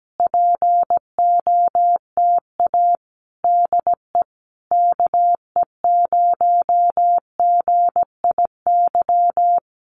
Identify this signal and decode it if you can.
Morse code, heard as a keyed tone, about 17 wpm, 700 Hz. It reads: POTA DE KE0GIY
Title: Common QSO Phrases Encore 17wpm